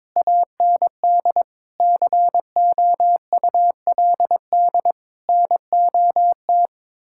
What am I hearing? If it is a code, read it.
Morse code, heard as a keyed tone, about 22 wpm, 700 Hz. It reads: AND COULD NOT
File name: Sentences with 7-Word Phrases - 22wpm